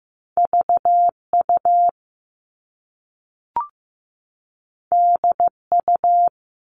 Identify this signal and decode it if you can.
Morse code, heard as a keyed tone, about 15 wpm, 700 Hz. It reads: VU DU
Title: All Cut Numbers - 2-Digit 15wpm